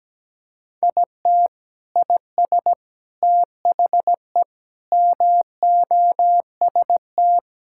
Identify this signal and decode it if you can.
Morse code, heard as a keyed tone, about 17 wpm, 700 Hz. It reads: IT IS THE MOST
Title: Sentences from Top 1000 Words - Review Part 6 - 17wpm